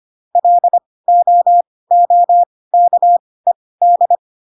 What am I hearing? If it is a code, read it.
Morse code, heard as a keyed tone, about 25 wpm, spaced as for 18 wpm, 700 Hz. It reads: LOOKED
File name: Pinocchio ch3 12 25 ik1hge